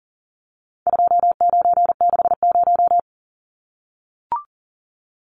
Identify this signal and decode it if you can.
Morse code, heard as a keyed tone, about 37 wpm, 700 Hz. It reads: 2960
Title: Numbers - 4-Digit 40wpm